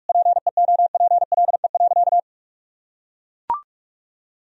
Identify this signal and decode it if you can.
Morse code, heard as a keyed tone, about 44 wpm, 700 Hz. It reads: PEOPLE.
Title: Sentences from Top 1000 Words - Review Part 3 - 45wpm